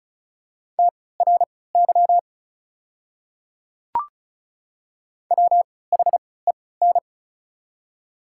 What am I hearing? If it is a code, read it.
Morse code, heard as a keyed tone, about 36 wpm, spaced as for 20 wpm, 700 Hz. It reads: TRY WHEN
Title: 3x Character Spacing - Top 200 Words - 35wpm